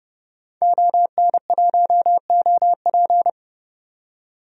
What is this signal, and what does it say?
Morse code, heard as a keyed tone, about 32 wpm, 700 Hz. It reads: ON1OP
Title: DXCC-100 Call Signs - 5 Characters - 30wpm